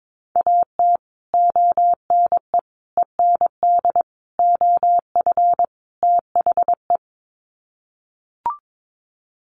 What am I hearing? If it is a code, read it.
Morse code, heard as a keyed tone, about 22 wpm, 700 Hz. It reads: AT ONE END OF THE
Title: Sets of 5 Words Encore 22wpm